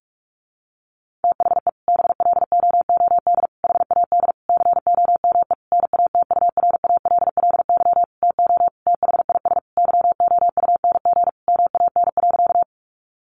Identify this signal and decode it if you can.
Morse code, heard as a keyed tone, about 46 wpm, 700 Hz. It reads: THE BLOOD HAD COME NATURALLY TO THIS YOUNG MAN.
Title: Sentences from Top 1300 Words 45wpm